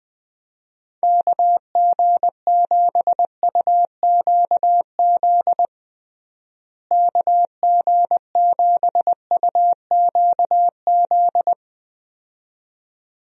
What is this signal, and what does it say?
Morse code, heard as a keyed tone, about 20 wpm, 700 Hz. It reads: KG7UQZ KG7UQZ
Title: USA Call Signs - 6 Characters - Repeated 2x - 20wpm